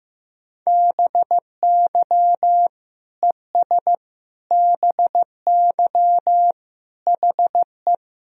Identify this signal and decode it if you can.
Morse code, heard as a keyed tone, about 16 wpm, 700 Hz. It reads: BY ES BY HE
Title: Sentences from Top 300 Words and Abbreviations 15wpm